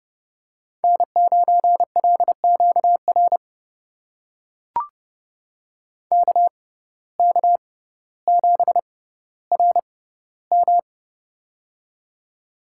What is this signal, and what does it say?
Morse code, heard as a keyed tone, about 30 wpm, spaced as for 15 wpm, 700 Hz. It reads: N9LQR KK7RM
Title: USA Call Signs - 5 Characters - 6x Character Spacing - 30wpm